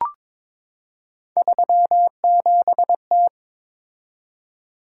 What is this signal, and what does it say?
Morse code, heard as a keyed tone, about 22 wpm, 700 Hz. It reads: 37T
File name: practice / 22wpm / Three Letter-Number Combinations - 22wpm